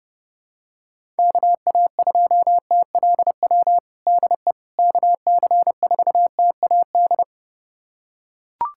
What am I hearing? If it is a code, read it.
Morse code, heard as a keyed tone, about 32 wpm, 700 Hz. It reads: KA2TLW DE KC4TAD
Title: USA Callsign DE Callsign - 6 Characters - 30wpm